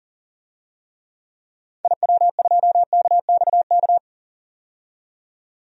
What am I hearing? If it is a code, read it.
Morse code, heard as a keyed tone, about 40 wpm, 700 Hz. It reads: IW2KXK
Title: DXCC-100 Call Signs - 6 Characters - 40wpm